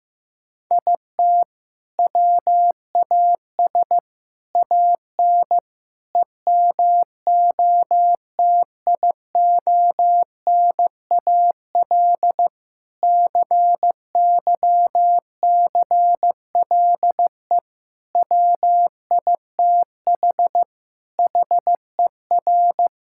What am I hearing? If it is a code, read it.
Morse code, heard as a keyed tone, about 15 wpm, 700 Hz. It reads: IT WAS AN EMOTIONAL CYCLE WITH HER
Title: Sentences from Top 1000 Words - Review Part 6 - 15wpm